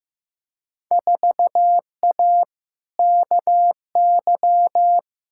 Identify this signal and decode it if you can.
Morse code, heard as a keyed tone, about 15 wpm, 700 Hz. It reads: 4A KY
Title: Field Day Exchanges Encore - 15wpm